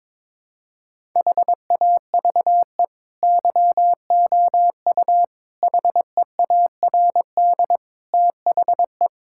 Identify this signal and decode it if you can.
Morse code, heard as a keyed tone, about 22 wpm, 700 Hz. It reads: HAVE YOU HEARD THE
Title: Sentences from Top 1000 Words - Review Part 5 - 22wpm